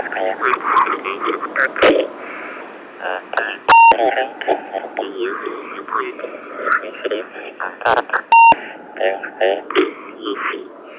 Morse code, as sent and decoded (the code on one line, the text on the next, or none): none